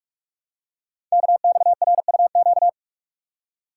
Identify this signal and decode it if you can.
Morse code, heard as a keyed tone, about 44 wpm, 700 Hz. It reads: KXRUQ
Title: Code Groups - 5 Letters - 45wpm